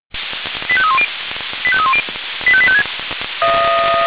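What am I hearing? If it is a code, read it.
Five-tone selective call (ZVEI), heard as quick runs of five tones, tones of about 70 ms. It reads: 85310 85310 85856
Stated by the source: ZVEI five-tone selective call received by SDR.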